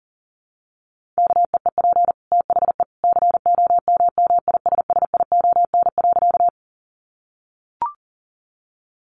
Morse code, heard as a keyed tone, about 40 wpm, 700 Hz. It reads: KEEP THE COMMISSION.
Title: Sentences from Top 1400 Words 40wpm